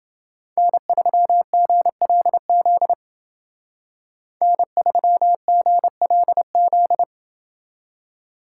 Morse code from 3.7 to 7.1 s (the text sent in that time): N3GLZ